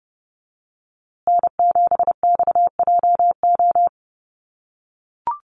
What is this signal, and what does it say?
Morse code, heard as a keyed tone, about 30 wpm, 700 Hz. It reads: N7XJO
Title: USA Call Signs Repeated 2x 30wpm